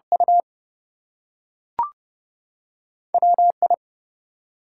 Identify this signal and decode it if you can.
Morse code, heard as a keyed tone, about 30 wpm, 700 Hz. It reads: U WI
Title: US States and Canadian Provinces 30wpm